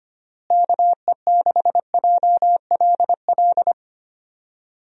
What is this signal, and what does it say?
Morse code, heard as a keyed tone, about 25 wpm, 700 Hz. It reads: KE6JLL
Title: USA Call Signs - 6 Characters - Repeated 3x - 25wpm